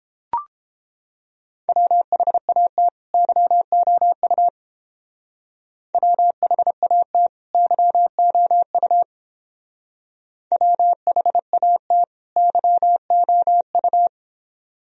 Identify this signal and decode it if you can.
Morse code, heard as a keyed tone, about 29 wpm, 700 Hz. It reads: WHAT YOU WHAT YOU WHAT YOU